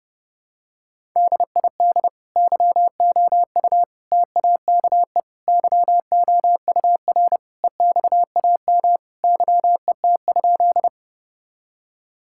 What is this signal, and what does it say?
Morse code, heard as a keyed tone, about 30 wpm, 700 Hz. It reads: DID YOU TAKE YOUR EXAM YET?
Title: Sentences from Top 1000 Words - Review Part 5 - 30wpm